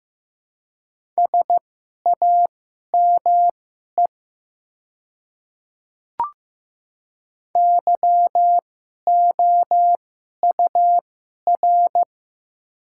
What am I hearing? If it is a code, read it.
Morse code, heard as a keyed tone, about 15 wpm, spaced as for 11 wpm, 700 Hz. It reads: SAME YOUR
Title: 2x Character Spacing - Top 300 Words - 15wpm